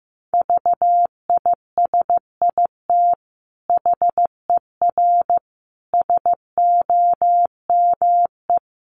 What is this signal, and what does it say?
Morse code, heard as a keyed tone, about 14 wpm, 700 Hz. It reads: VISIT HER SOME
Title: Sentences from Top 900 Words 15wpm